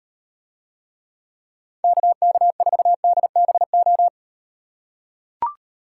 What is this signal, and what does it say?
Morse code, heard as a keyed tone, about 38 wpm, 700 Hz. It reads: KK4DBO